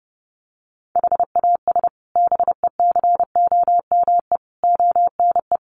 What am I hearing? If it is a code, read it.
Morse code, heard as a keyed tone, about 30 wpm, 700 Hz. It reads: HAS BECOME ONE
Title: Sets of 5 Words Encore 30wpm